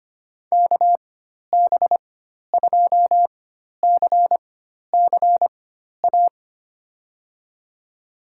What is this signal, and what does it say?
Morse code, heard as a keyed tone, about 25 wpm, spaced as for 12 wpm, 700 Hz. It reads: KB2CCA
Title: USA Call Signs - 6 Characters - 4x Character Spacing - 25wpm